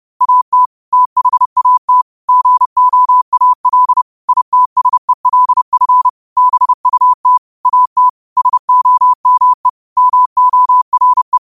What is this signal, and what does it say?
Morse code, heard as a keyed tone, about 30 wpm, 1000 Hz. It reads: AT THAT GOAL ITSELF BUT AT SOME MORE
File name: qotdfiles / 30wpm